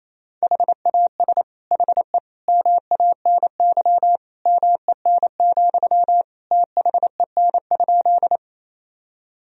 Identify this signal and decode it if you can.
Morse code, heard as a keyed tone, about 28 wpm, 700 Hz. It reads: HAS HE MANY MEN, THEN?
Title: Sentences from Top 1000 Words - Review Part 1 - 28wpm